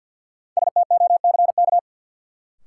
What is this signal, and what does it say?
Morse code, heard as a keyed tone, about 52 wpm, 700 Hz. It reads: STOCK
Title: Sentences from Top 800 Words - New Word Drill - 50wpm